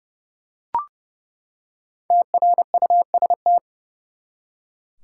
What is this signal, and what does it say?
Morse code, heard as a keyed tone, about 30 wpm, 700 Hz. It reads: TRUST
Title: Five Letter Words - 30wpm